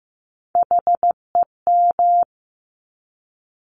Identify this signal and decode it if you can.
Morse code, heard as a keyed tone, about 15 wpm, 700 Hz. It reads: HEM
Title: Top Three Letter Combinations in Words 15wpm